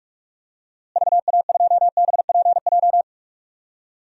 Morse code, heard as a keyed tone, about 46 wpm, 700 Hz. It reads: VA2BPJ
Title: DXCC-100 Call Signs - 6 Characters - Repeated 3x - 45wpm